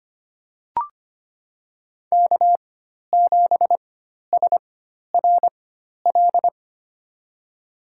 Morse code, heard as a keyed tone, about 25 wpm, spaced as for 12 wpm, 700 Hz. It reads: K7SRL
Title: USA Call Signs - 5 Characters - 4x Character Spacing - 25wpm